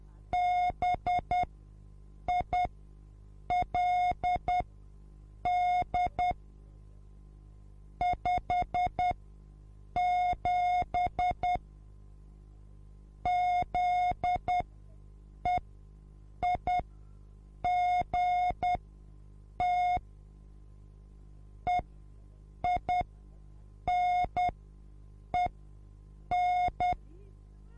Morse code, heard as a keyed tone, about 10 wpm, spaced as for 7 wpm, 720 Hz. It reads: BILD 57 ZEIGT EINEN
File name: repos / Morsekurs / DJ4UF Morsen24